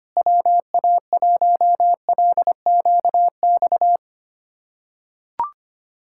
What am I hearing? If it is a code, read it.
Morse code, heard as a keyed tone, about 25 wpm, 700 Hz. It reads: WA1LQX
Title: USA Callsign DE Callsign - 6 Characters - 25wpm